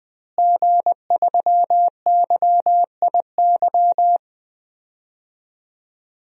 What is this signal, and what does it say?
Morse code, heard as a keyed tone, about 20 wpm, 700 Hz. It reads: G3YIY
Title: DXCC-100 Call Signs - 5 Characters - Repeated 2x - 20wpm